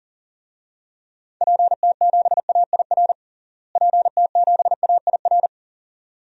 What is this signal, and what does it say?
Morse code, heard as a keyed tone, about 37 wpm, 700 Hz. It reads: PT7AIR PT7AIR
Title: DXCC-100 Call Signs - 6 Characters - Repeated 2x - 40wpm